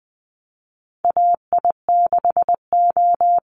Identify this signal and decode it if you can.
Morse code, heard as a keyed tone, about 20 wpm, 700 Hz. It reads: AI6O
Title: Call Signs Repeated 3x - 20wpm